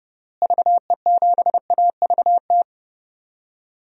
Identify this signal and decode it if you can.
Morse code, heard as a keyed tone, about 30 wpm, 700 Hz. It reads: VE7AVT
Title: DXCC-100 Call Signs - 6 Characters - Repeated 3x - 30wpm